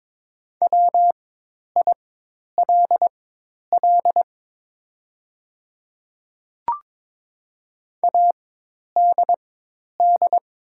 Morse code, heard as a keyed tone, about 22 wpm, spaced as for 10 wpm, 700 Hz. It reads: WILL ADD
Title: 4x Character Spacing - Top 200 Words - 22wpm